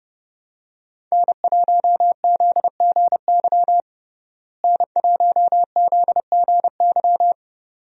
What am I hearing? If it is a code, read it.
Morse code, heard as a keyed tone, about 32 wpm, 700 Hz. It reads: N1ZGY N1ZGY
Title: USA Call Signs - 5 Characters - Repeated 2x - 30wpm